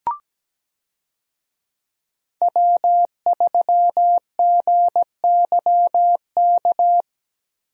Morse code, heard as a keyed tone, about 17 wpm, 700 Hz. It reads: W3GYK
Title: USA Call Signs - 5 Characters - Repeated 3x - 17wpm